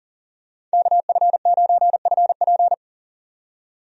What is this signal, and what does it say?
Morse code, heard as a keyed tone, about 40 wpm, 700 Hz. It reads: KF9FP
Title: USA Call Signs - 5 Characters - 40wpm